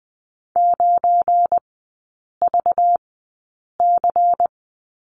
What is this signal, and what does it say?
Morse code, heard as a keyed tone, about 20 wpm, spaced as for 8 wpm, 690 Hz. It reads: 9VC
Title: 198 Lesson 32e - Letter k - Three Characters 20wpm